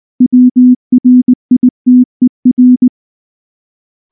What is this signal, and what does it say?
Morse code, heard as a keyed tone, about 19 wpm, 250 Hz. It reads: WRITER
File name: morse / morse-training-20WPM